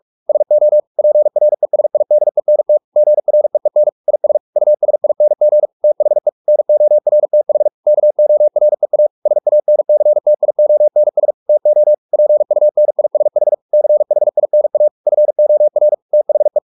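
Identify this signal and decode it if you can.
Morse code, heard as a keyed tone, about 44 wpm, 600 Hz. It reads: SO PRESIDENT GREEN IS USING THE NORTH KOREA SANCTIONS TO PUNISH CHINA FOR THE
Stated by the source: Morse practice at 45 wpm, the novel Never by Ken Follett in English, from CW med Gnister.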